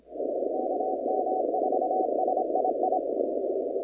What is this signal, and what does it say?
Morse code, heard as a keyed tone, about 26 wpm, 740 Hz, with noise. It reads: KM4SII
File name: contests / 2019-cw-wpx-kt5j-k5pi- -2330 0-20